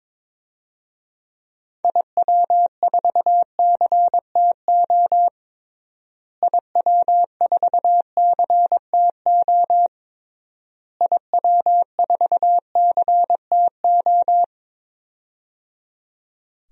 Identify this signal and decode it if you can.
Morse code, heard as a keyed tone, about 22 wpm, 700 Hz. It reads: IW4CTO IW4CTO IW4CTO
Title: DXCC-100 Call Signs - 6 Characters - Repeated 3x - 22wpm